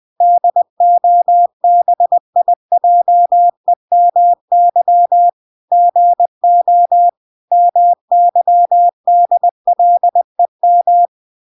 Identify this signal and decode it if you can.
Morse code, heard as a keyed tone, about 20 wpm, 700 Hz. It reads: DOBIJEMY GO MYDLEM